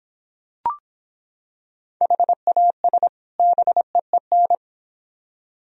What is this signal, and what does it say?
Morse code, heard as a keyed tone, about 26 wpm, 700 Hz. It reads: HAS BEEN